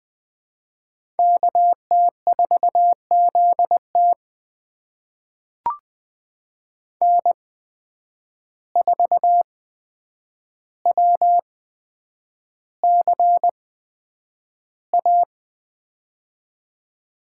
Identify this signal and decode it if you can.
Morse code, heard as a keyed tone, about 21 wpm, spaced as for 9 wpm, 700 Hz. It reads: KT4ZT N4WCA
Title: USA Call Signs - 5 Characters - 8x Character Spacing - 20wpm